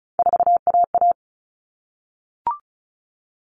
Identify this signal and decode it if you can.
Morse code, heard as a keyed tone, about 34 wpm, 700 Hz. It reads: 4AA
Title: All Cut Numbers - 3-Digit 35wpm